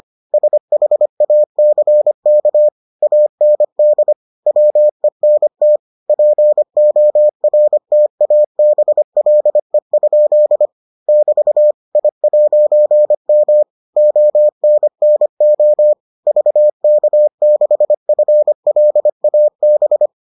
Morse code, heard as a keyed tone, about 25 wpm, 600 Hz. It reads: SHACK AND WENT PORTABLE? <BT> I'M ONNO VK6FLAB